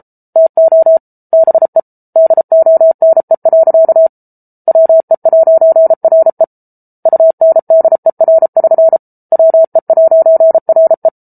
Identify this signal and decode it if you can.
Morse code, heard as a keyed tone, about 34 wpm, 650 Hz. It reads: TO BE DONE. WE'RE UNDER<SN> WE'RE